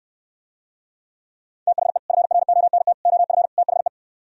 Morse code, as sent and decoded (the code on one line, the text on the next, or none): - .... / ..-. .-. --- -. - / --- ..-. / - .... .
THE FRONT OF THE